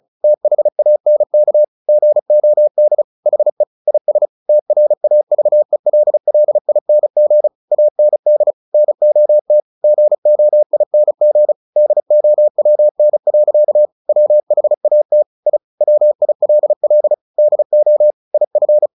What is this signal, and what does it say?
Morse code, heard as a keyed tone, about 34 wpm, 600 Hz. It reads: THANK GOD HE IS TRAVELLING AND NOT GOING DOWN. WHAT I WILL DO IF
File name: the old man and the sea / the old man and the sea 1 35wpm